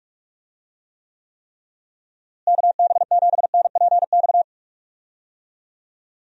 Morse code, heard as a keyed tone, about 46 wpm, 700 Hz. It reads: KB7NPX
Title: USA Call Signs - 6 Characters - Repeated 3x - 45wpm